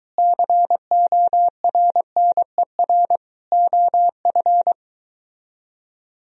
Morse code, heard as a keyed tone, about 23 wpm, 700 Hz. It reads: CORNER OF